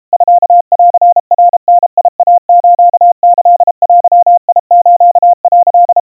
Morse code (Japanese Wabun code), heard as a keyed tone, about 32 wpm, 700 Hz. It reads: ミンナダイスキデス」
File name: kawa / Morse sample / kamyu 120 cpm 700 Hz Crispy